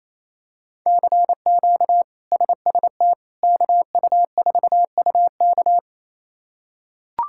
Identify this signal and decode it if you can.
Morse code, heard as a keyed tone, about 28 wpm, 700 Hz. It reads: CQ SST KU4UK
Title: K1USN SST QSO Practice - 28wpm